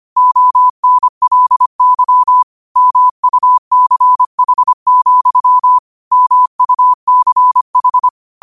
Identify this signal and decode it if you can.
Morse code, heard as a keyed tone, about 25 wpm, 1000 Hz. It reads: ONLY MUCH, MUCH
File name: qotdfiles / 25wpm